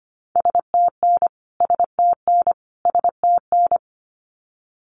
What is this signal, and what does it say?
Morse code, heard as a keyed tone, about 25 wpm, 700 Hz. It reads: STN STN STN
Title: Abbreviations Repeated 3x - 25wpm